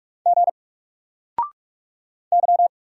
Morse code, heard as a keyed tone, about 44 wpm, 700 Hz. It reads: G Y
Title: Single Letter-Number 45wpm